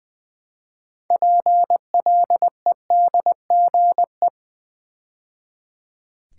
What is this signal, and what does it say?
Morse code, heard as a keyed tone, about 20 wpm, 700 Hz. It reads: PLEDGE
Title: Build A Word - 2 Letters At A Time - 6-Letter Words - 20wpm